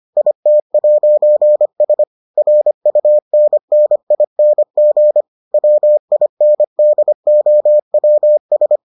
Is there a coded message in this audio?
Yes: Morse code from 0.2 to 8.8 s, heard as a keyed tone, about 25 wpm, 600 Hz. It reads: IT'S RUNNING WINDOWS